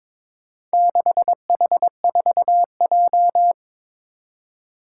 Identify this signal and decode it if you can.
Morse code, heard as a keyed tone, about 23 wpm, 700 Hz. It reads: 6H4J
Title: Four Challenging Letter-Number Combinations 22wpm